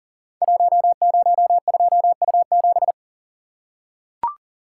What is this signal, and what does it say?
Morse code, heard as a keyed tone, about 39 wpm, 700 Hz. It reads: 102U7
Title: Code Groups - 5 Letter:Numbers - 40wpm